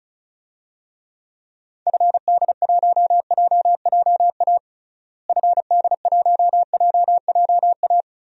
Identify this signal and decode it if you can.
Morse code, heard as a keyed tone, about 36 wpm, 700 Hz. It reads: FD1JJA FD1JJA
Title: DXCC-100 Call Signs - 6 Characters - Repeated 3x - 35wpm